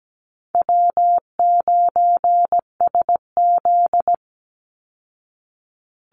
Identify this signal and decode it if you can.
Morse code, heard as a keyed tone, about 17 wpm, 700 Hz. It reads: W9SZ
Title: Easy Call Signs Repeated 3x - 17wpm